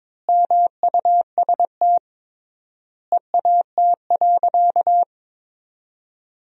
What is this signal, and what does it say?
Morse code, heard as a keyed tone, about 22 wpm, 700 Hz. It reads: MUST EAT.
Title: 3x Word-Spacing - Sentences from Top 300 Words - 22wpm